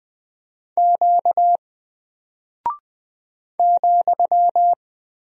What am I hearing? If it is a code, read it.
Morse code, heard as a keyed tone, about 20 wpm, 700 Hz. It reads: Q ,
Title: Single Character (Letter:Number:Punctuation:Prosign) 20wpm